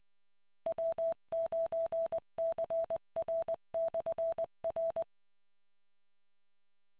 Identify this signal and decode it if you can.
Morse code, heard as a keyed tone, about 20 wpm, 660 Hz. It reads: W9CR/R